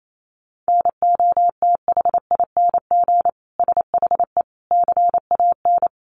Morse code, heard as a keyed tone, about 28 wpm, 700 Hz. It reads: NOTHING SHE CAN
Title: Sentences from Top 500 Words 28wpm